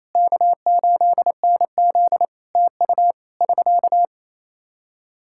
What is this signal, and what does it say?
Morse code, heard as a keyed tone, about 28 wpm, 700 Hz. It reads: K8NZ TU <SK>
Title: Practice QSO - Exchange 4 - Ending - 28wpm